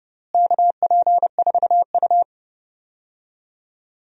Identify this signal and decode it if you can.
Morse code, heard as a keyed tone, about 30 wpm, 700 Hz. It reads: KP4U
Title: DXCC-100 Call Signs - 4 Characters - Repeated 2x - 30wpm